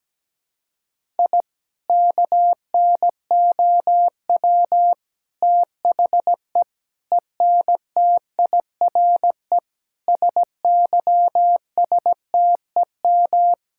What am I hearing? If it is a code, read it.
Morse code, heard as a keyed tone, about 17 wpm, 700 Hz. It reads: I KNOW THE ENTIRE SYSTEM
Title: Sentences from Top 1000 Words - Review Part 5 - 17wpm